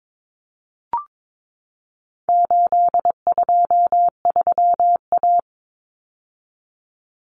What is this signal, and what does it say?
Morse code, heard as a keyed tone, about 22 wpm, 700 Hz. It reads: 823A
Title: Common Cut Numbers - 4-Digit 22wpm